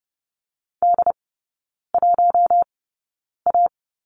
Morse code, heard as a keyed tone, about 30 wpm, spaced as for 9 wpm, 700 Hz. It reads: D1A
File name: Lesson 18e - Letter m - Three Characters 30wpm